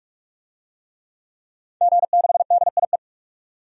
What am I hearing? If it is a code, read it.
Morse code, heard as a keyed tone, about 46 wpm, 700 Hz. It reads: G6DIE
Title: DXCC-100 Call Signs - 5 Characters - 45wpm